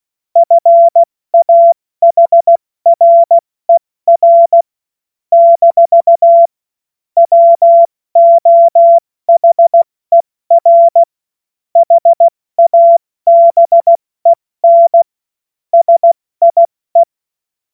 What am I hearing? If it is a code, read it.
Morse code, heard as a keyed tone, about 16 wpm, 680 Hz. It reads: FAHRER - WOHER HABEN SIE
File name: Morsekurs / Witze / Witz 05. Oktober 2021, 70 F80